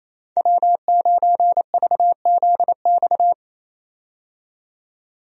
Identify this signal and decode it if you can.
Morse code, heard as a keyed tone, about 27 wpm, 700 Hz. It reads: W9VZX
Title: USA Call Signs - 5 Characters - 28wpm